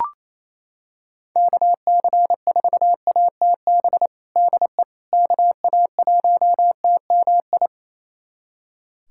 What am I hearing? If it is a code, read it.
Morse code, heard as a keyed tone, about 28 wpm, 700 Hz. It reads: KC4ATB DE KA1TMI
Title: USA Callsign DE Callsign - 6 Characters - 28wpm